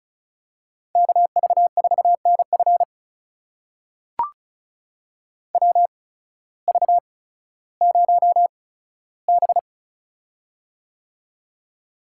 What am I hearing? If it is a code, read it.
Morse code, heard as a keyed tone, about 36 wpm, spaced as for 16 wpm, 700 Hz. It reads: KV4NF WV0B
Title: USA Call Signs - 8x Character Spacing - 35wpm